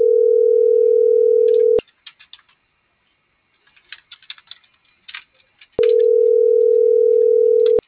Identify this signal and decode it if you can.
Ringback tone (North America), 440 + 480 Hz.